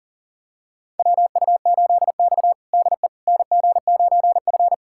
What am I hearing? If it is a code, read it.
Morse code, heard as a keyed tone, about 40 wpm, 700 Hz. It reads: WU8X DE NG9F
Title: USA Callsign DE Callsign - 4 Characters - 40wpm